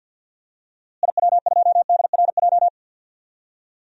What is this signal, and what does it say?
Morse code, heard as a keyed tone, about 51 wpm, 700 Hz. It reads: IW2DRJ
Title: DXCC-100 Call Signs - 6 Characters - Repeated 3x - 50wpm